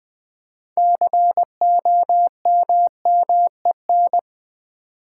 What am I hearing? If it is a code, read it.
Morse code, heard as a keyed tone, about 20 wpm, 700 Hz. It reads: COMMEN
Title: Build A Word - 2 Letters At A Time - 8-Letter Words - 20wpm